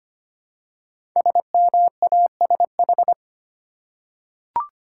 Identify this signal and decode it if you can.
Morse code, heard as a keyed tone, about 25 wpm, 700 Hz. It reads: SMASH
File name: 5 Letter Words - First 3 Letters Given - 25wpm